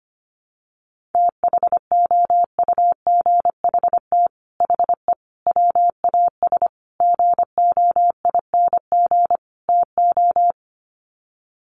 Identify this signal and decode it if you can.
Morse code, heard as a keyed tone, about 25 wpm, 700 Hz. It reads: THOUGHT HE WAS GOING TO
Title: Sets of 5 Words Encore2 25wpm